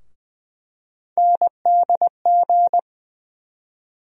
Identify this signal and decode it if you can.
Morse code, heard as a keyed tone, about 20 wpm, 700 Hz. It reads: NDG
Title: USA Call Sign Postfixes - 3 Letters 20wpm